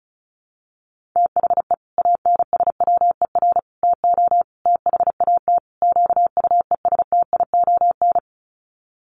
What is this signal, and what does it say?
Morse code, heard as a keyed tone, about 34 wpm, 700 Hz. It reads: THE ANSWER TO THAT QUESTION
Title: Sets of 5 Words Encore2 35wpm